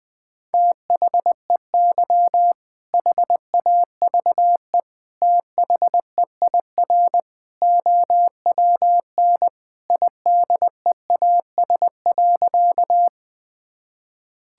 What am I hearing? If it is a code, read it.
Morse code, heard as a keyed tone, about 20 wpm, 700 Hz. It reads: THEY HAVE THEIR OWN IDEAS.
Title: Sentences with 2-Word Phrases - Spoken Beginning - 20wpm